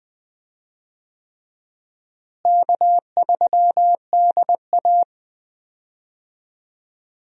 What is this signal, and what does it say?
Morse code, heard as a keyed tone, about 21 wpm, 700 Hz. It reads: K3DA
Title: USA Call Signs - 4 Characters - Repeated 3x - 20wpm